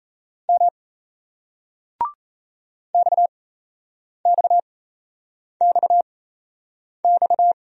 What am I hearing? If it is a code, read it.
Morse code, heard as a keyed tone, about 35 wpm, spaced as for 8 wpm, 700 Hz. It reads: M XXXX
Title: Speed-Racing Single Letters 28wpm